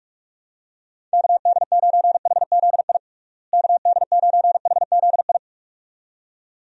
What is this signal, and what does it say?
Morse code, heard as a keyed tone, about 46 wpm, 700 Hz. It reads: KD9HZI KD9HZI
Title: USA Call Signs - 6 Characters - Repeated 2x - 45wpm